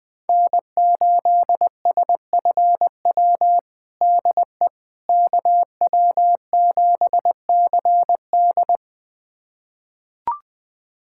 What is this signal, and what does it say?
Morse code, heard as a keyed tone, about 20 wpm, 700 Hz. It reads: N8SFW DE KW7CD
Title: USA Callsign DE Callsign - 5 Characters - 20wpm